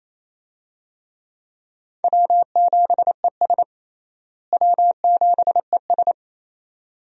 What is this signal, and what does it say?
Morse code, heard as a keyed tone, about 28 wpm, 700 Hz. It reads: W7ES W7ES
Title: USA Call Signs - 4 Characters - Repeated 2x - 28wpm